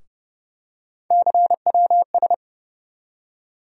Morse code, heard as a keyed tone, about 30 wpm, 700 Hz. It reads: CWS
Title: USA Call Sign Postfixes - 3 Letters 30wpm